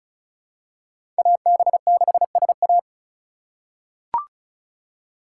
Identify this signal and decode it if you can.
Morse code, heard as a keyed tone, about 34 wpm, 700 Hz. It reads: AB6SA
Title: USA Call Signs - 5 Characters - 3x Character Spacing - 35wpm